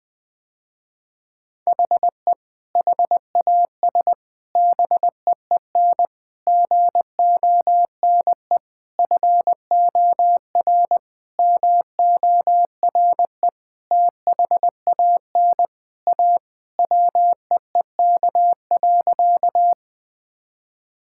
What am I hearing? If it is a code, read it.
Morse code, heard as a keyed tone, about 21 wpm, 700 Hz. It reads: HE HAS BEEN GONE FOR MORE THAN A WEEK.
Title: Sentences with 4-Word Phrases - Spoken Beginning - 20wpm